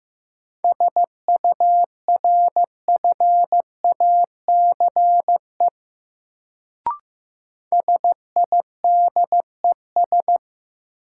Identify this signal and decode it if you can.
Morse code, heard as a keyed tone, about 15 wpm, 700 Hz. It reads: SURFACE SIDES
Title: Sets of 4 Words - New Word Drill - 15wpm